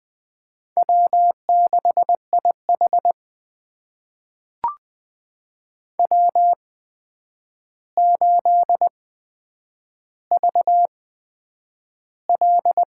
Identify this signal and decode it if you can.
Morse code, heard as a keyed tone, about 20 wpm, spaced as for 9 wpm, 700 Hz. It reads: W6IH W8VL
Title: USA Call Signs - 8x Character Spacing - 20wpm